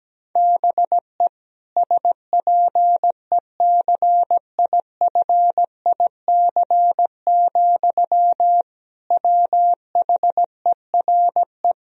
Morse code, heard as a keyed tone, about 17 wpm, 700 Hz. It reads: BE SPECIFIC, WHERE
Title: Sentences from Top 1000 Words - Review Part 3 - 17wpm